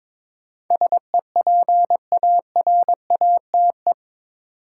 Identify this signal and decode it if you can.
Morse code, heard as a keyed tone, about 22 wpm, 700 Hz. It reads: SEPARATE
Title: Sentences from Top 1000 Words - Review Part 7 - New Word Drill - 22wpm